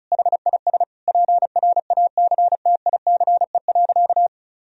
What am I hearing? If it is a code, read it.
Morse code, heard as a keyed tone, about 36 wpm, 700 Hz. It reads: HIS PRACTICE.
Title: Sentences from Top 1000 Words - Review Part 4 - 35wpm